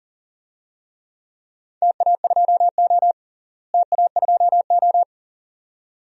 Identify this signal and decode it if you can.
Morse code, heard as a keyed tone, about 40 wpm, 700 Hz. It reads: TA2O TA2O
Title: DXCC-100 Call Signs - 4 Characters - Repeated 2x - 40wpm